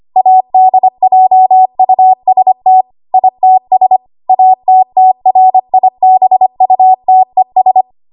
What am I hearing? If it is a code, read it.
Morse code, heard as a keyed tone, about 25 wpm, 750 Hz. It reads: ADJUST ITS ATTRIBUTES
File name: Archive / 25 WPM / 25WPM